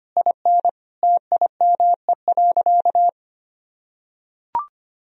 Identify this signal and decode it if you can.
Morse code, heard as a keyed tone, about 25 wpm, 700 Hz. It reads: IN TIME.
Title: Sentences from Top 1000 Words - Review Part 2 - 25wpm